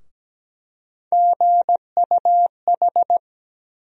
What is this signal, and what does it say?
Morse code, heard as a keyed tone, about 17 wpm, 700 Hz. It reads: GUH
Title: USA Call Sign Postfixes - 3 Letters 17wpm